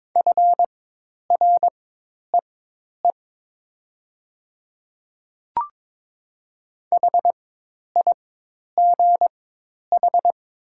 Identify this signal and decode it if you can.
Morse code, heard as a keyed tone, about 22 wpm, spaced as for 10 wpm, 700 Hz. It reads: FREE HIGH
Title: 4x Character Spacing - Top 500 Words - 22wpm